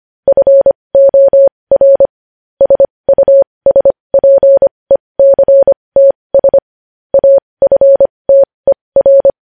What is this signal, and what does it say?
Morse code, heard as a keyed tone, about 25 wpm, spaced as for 20 wpm, 550 Hz. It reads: FOR SUSPECTS AFTER
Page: news-headlines-in-morse-code-at-20-wpm